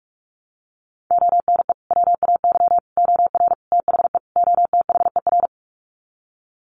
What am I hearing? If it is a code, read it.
Morse code, heard as a keyed tone, about 45 wpm, 700 Hz. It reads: ONE WAY OR THE OTHER
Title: Sets of 5 Words 45wpm